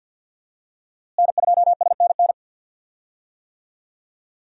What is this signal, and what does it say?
Morse code, heard as a keyed tone, about 47 wpm, 700 Hz. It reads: N2SNN
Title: USA Call Signs - 5 Characters - Repeated 3x - 50wpm